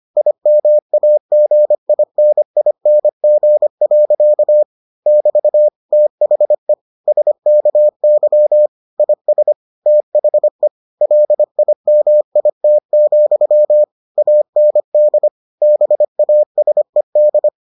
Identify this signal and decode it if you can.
Morse code, heard as a keyed tone, about 25 wpm, 600 Hz. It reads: IMAGINING. <BT> THE SKY IS THE LIMIT, AND BASED